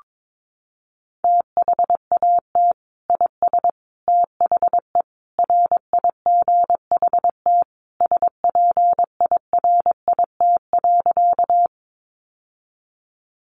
Morse code, heard as a keyed tone, about 22 wpm, 700 Hz. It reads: THAT IS THE RIGHT SPIRIT.
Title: Sentences from Top 1400 Words 22wpm